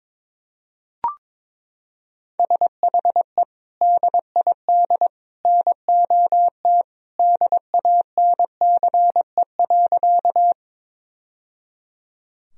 Morse code, heard as a keyed tone, about 22 wpm, 700 Hz. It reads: SHE DID NOT DANCE.
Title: Sentences from Top 1000 Words - Review Part 4 - 22wpm